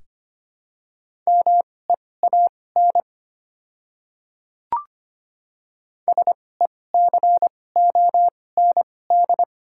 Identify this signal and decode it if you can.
Morse code, heard as a keyed tone, about 25 wpm, spaced as for 18 wpm, 700 Hz. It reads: MEAN SECOND
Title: 2x Character Spacing - Top 300 Words - 25wpm